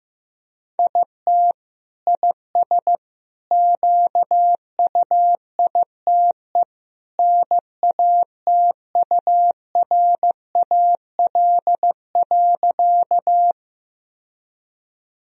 Morse code, heard as a keyed tone, about 15 wpm, 700 Hz. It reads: IT IS QUITE NATURAL.
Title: Sentences from Top 1000 Words - Review Part 4 - 15wpm